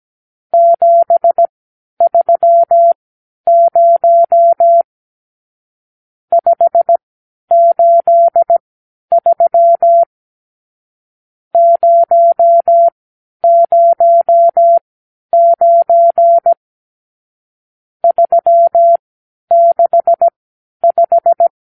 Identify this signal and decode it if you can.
Morse code, heard as a keyed tone, about 17 wpm, spaced as for 10 wpm, 680 Hz. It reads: 730 583 009 365
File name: arrl / Review (1-0)